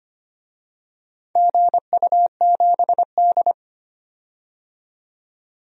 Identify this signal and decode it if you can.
Morse code, heard as a keyed tone, about 25 wpm, 700 Hz. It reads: GU7D
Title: DXCC-100 Call Signs - 4 Characters - 25wpm